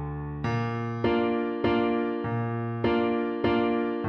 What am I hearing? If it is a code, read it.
There is no code.